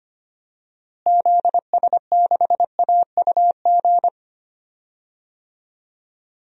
Morse code, heard as a keyed tone, about 25 wpm, 700 Hz. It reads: ZS6AUG
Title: DXCC-100 Call Signs - 6 Characters - Repeated 2x - 25wpm